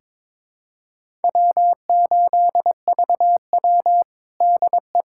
Morse code, heard as a keyed tone, about 21 wpm, 700 Hz. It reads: W8VW DE
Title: USA Callsign DE Callsign - 4 Characters - 22wpm